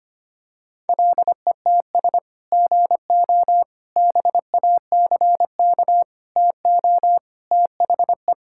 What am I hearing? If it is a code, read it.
Morse code, heard as a keyed tone, about 25 wpm, 700 Hz. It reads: LETS GO BACK TO THE
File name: practice / 25wpm / Sentences with 3-Word Phrases - 25wpm